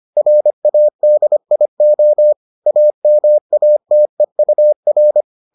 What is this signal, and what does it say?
Morse code, heard as a keyed tone, about 25 wpm, 600 Hz. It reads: RADIO AMATEUR